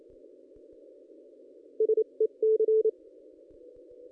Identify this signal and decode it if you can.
Morse code, heard as a keyed tone, about 28 wpm, spaced as for 25 wpm, 430 Hz. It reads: SEC
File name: contests / ss-cw-k5tr-op-n5rz -1400